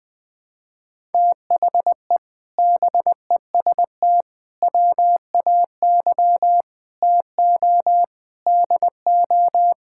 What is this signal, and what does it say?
Morse code, heard as a keyed tone, about 20 wpm, 700 Hz. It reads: THE BEST WAY TO DO
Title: Sentences with 8-Word Phrases - 20wpm